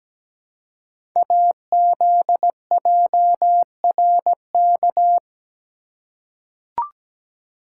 Morse code, heard as a keyed tone, about 17 wpm, 700 Hz. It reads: AZJRK
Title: Code Groups - 5 Letter:Numbers - 17wpm